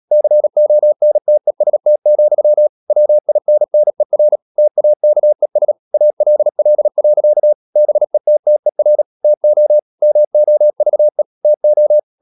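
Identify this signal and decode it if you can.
Morse code, heard as a keyed tone, about 37 wpm, 600 Hz. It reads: CONTEST, WINNER TAKES ALL. BETTER TO MOVE TO
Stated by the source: Morse code practice at 35–39 wpm sending English text of Ken Follett's novel Never, from CW med Gnister.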